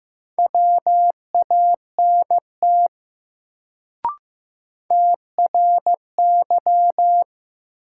Morse code, heard as a keyed tone, about 15 wpm, 700 Hz. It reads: WANT TRY
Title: Sentences from Top 200 Words - New Word Drill - 15wpm